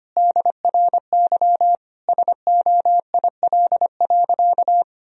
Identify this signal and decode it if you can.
Morse code, heard as a keyed tone, about 25 wpm, 700 Hz. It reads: DRY SOIL.
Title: Sentences from Top 1000 Words - Review Part 4 - 25wpm